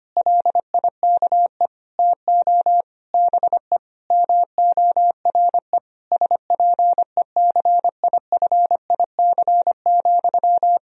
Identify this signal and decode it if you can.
Morse code, heard as a keyed tone, about 25 wpm, 700 Hz. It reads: LIKE TO BE MORE SPECIFIC,
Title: Sentences from Top 1000 Words - Review Part 3 - 25wpm